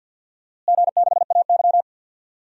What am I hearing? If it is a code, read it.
Morse code, heard as a keyed tone, about 50 wpm, 700 Hz. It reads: G6AY
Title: DXCC-100 Call Signs - 4 Characters - Repeated 3x - 50wpm